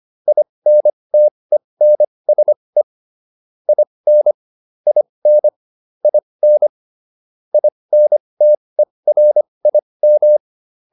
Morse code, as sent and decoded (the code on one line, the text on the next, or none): .. -. - . -. ... / .. -. / .. -. / .. -. / .. -. - . .-. .. --
INTENSE IN IN IN INTERIM